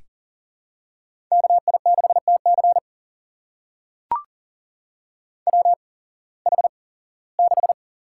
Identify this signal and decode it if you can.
Morse code, heard as a keyed tone, about 40 wpm, spaced as for 21 wpm, 700 Hz. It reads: KI6TC WH6
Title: USA Call Signs - 5 Characters - 8x Character Spacing - 40wpm